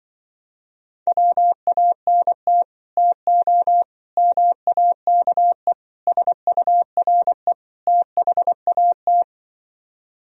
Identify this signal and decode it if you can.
Morse code, heard as a keyed tone, about 25 wpm, 700 Hz. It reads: WANT TO MAKE SURE THAT